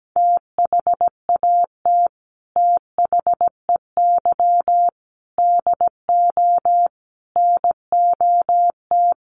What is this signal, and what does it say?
Morse code, heard as a keyed tone, about 17 wpm, 700 Hz. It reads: THAT THEY DO NOT
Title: Sets of 5 Words Encore2 17wpm